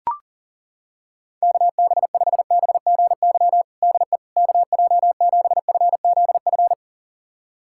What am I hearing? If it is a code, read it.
Morse code, heard as a keyed tone, about 40 wpm, 700 Hz. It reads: KB5BGY DE KJ7FZF
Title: USA Callsign DE Callsign 40wpm